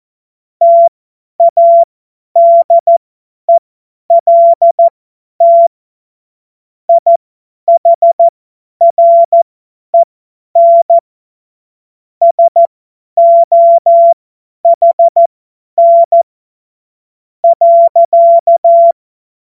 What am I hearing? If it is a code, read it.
Morse code, heard as a keyed tone, about 14 wpm, spaced as for 10 wpm, 680 Hz. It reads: TADELT IHREN SOHN .
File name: Morsekurs / Witze / Witz 27. September 2022 50 F70